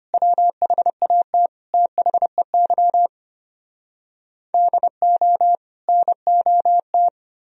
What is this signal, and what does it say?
Morse code, heard as a keyed tone, about 33 wpm, 700 Hz. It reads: WHAT THEY DO NOT